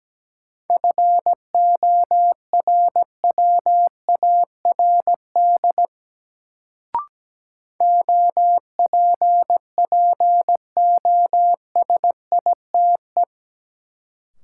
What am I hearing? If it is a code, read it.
Morse code, heard as a keyed tone, about 17 wpm, 700 Hz. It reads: FORWARD OPPOSITE
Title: Sentences from Top 1300 Words - New Word Drill - 17wpm